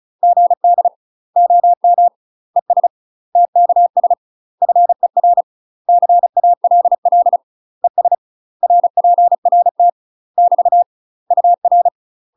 Morse code, heard as a keyed tone, about 34 wpm, 700 Hz. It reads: GD OM ES TKS FER CALL ES RPRT <BT> UR
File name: UK MASTER QSOs TRAINING / 35WPM / UK MASTER QSOs 0005